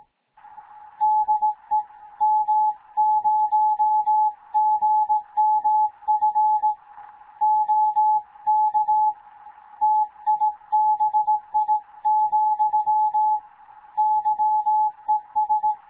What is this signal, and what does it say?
Morse code, heard as a keyed tone, about 18 wpm, 860 Hz. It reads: DE M0GMF OK TIBI, YES
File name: Morze-HA2MN / hangp / 2 11 GYAKORLAT